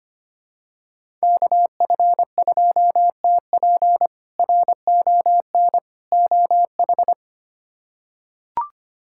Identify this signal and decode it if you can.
Morse code, heard as a keyed tone, about 25 wpm, 700 Hz. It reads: KF2TP RON OH
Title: K1USN SST QSO Practice - 25wpm